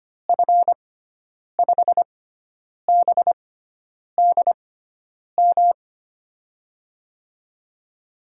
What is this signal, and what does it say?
Morse code, heard as a keyed tone, about 25 wpm, spaced as for 9 wpm, 700 Hz. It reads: F5BDM
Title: USA Call Signs - 6 Characters - 6x Character Spacing - 25wpm